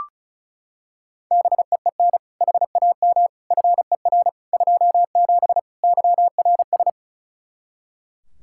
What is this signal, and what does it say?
Morse code, heard as a keyed tone, about 36 wpm, 700 Hz. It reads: BEEN HAM FER 27 YRS
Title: Practice QSO - Exchange 3 - More Info - 35wpm